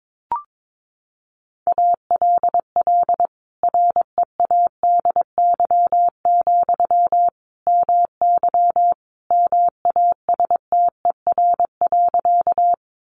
Morse code, heard as a keyed tone, about 22 wpm, 700 Hz. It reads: ALL READY, MY MASTER.
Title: Sentences from Top 1300 Words 22wpm